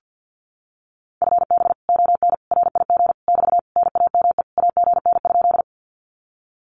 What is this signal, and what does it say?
Morse code, heard as a keyed tone, about 50 wpm, 700 Hz. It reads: FB ON RIG <BT> NAME AGN?
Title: Common QSO Phrases Encore 50wpm